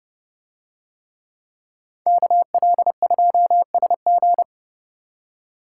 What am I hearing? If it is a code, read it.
Morse code, heard as a keyed tone, about 30 wpm, 700 Hz. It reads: KL2SG